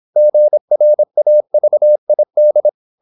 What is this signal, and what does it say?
Morse code, heard as a keyed tone, about 26 wpm, 600 Hz. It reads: GRAVID
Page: Never 26wpm | CW med Gnister